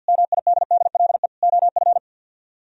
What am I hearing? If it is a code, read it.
Morse code, heard as a keyed tone, about 64 wpm, 700 Hz. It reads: MIDDLE OF